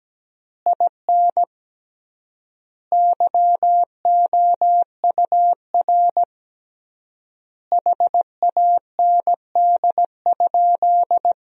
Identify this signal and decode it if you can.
Morse code, heard as a keyed tone, about 17 wpm, 700 Hz. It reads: IN YOUR HAND?
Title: 3x Word-Spacing - Sentences from Top 200 Words - 17wpm